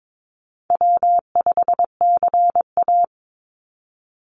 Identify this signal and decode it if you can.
Morse code, heard as a keyed tone, about 22 wpm, 700 Hz. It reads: W5CA
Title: USA Call Signs 22wpm